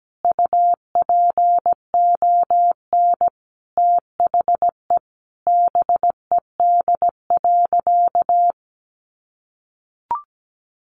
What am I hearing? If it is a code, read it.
Morse code, heard as a keyed tone, about 17 wpm, 700 Hz. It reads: UPON THE BED.
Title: Sentences from Top 1100 Words 17wpm